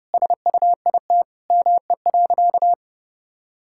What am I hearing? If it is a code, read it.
Morse code, heard as a keyed tone, about 30 wpm, 700 Hz. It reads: SUIT ME.
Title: Sentences from Top 1600 Words 30wpm